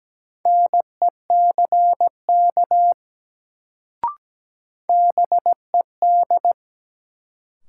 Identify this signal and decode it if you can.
Morse code, heard as a keyed tone, about 17 wpm, 700 Hz. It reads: NECK BED
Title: Sets of 5 Words Encore - New Word Drill - 17wpm